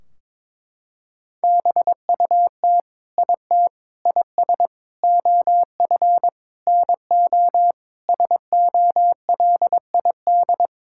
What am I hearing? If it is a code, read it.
Morse code, heard as a keyed tone, about 22 wpm, 700 Hz. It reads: BUT IT IS OF NO SOLID
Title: Sentences from Top 1600 Words 22wpm